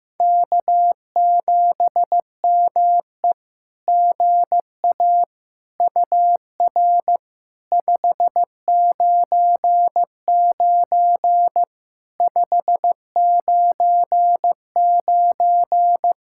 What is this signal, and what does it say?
Morse code, heard as a keyed tone, about 15 wpm, 700 Hz. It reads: K7ME GA UR 599 599
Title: SOTA QSO Practice 15wpm